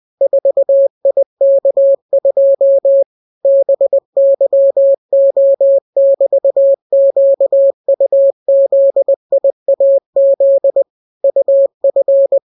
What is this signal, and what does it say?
Morse code, heard as a keyed tone, about 20 wpm, 550 Hz. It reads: 4IK2 BYO<BT>QUZIAZ UF